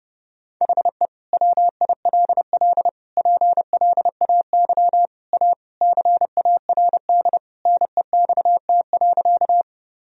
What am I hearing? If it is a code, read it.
Morse code, heard as a keyed tone, about 32 wpm, 700 Hz. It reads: HE WILL PLAY A CARD NEXT.
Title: Sentences from Top 1000 Words - Review Part 4 - 30wpm